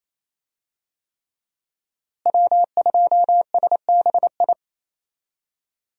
Morse code, heard as a keyed tone, about 28 wpm, 700 Hz. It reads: W2SBI
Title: USA Call Signs - 5 Characters - Repeated 3x - 28wpm